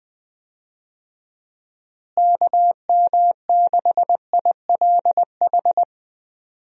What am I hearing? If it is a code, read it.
Morse code, heard as a keyed tone, about 20 wpm, 700 Hz. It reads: KM6ILH
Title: USA Call Signs - 6 Characters - Repeated 2x - 20wpm